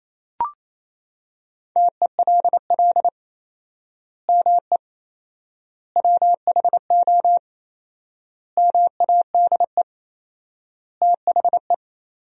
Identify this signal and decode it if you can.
Morse code, heard as a keyed tone, about 28 wpm, 700 Hz. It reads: TELL ME WHO MADE THE
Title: 4x Word-Spacing - Sentences from Top 200 Words - 28wpm